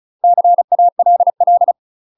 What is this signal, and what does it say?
Morse code, heard as a keyed tone, about 36 wpm, 700 Hz. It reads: CALL
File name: UK MASTER QSOs TRAINING / 35WPM / UK MASTER QSOs 0026